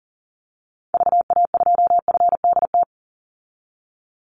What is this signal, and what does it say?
Morse code, heard as a keyed tone, about 38 wpm, 700 Hz. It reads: VA2FDT
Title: Call Signs - 40wpm